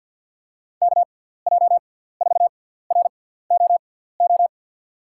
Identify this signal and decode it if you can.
Morse code, heard as a keyed tone, about 52 wpm, spaced as for 17 wpm, 700 Hz. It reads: KJ4ROO
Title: USA Call Signs - 6 Characters - 6x Character Spacing - 50wpm